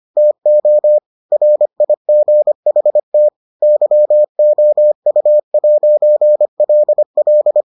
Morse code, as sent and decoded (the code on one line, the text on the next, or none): - --- / .-. .. --. .... - / -.-- --- ..- .----. .-.. .-..
TO RIGHT YOU'LL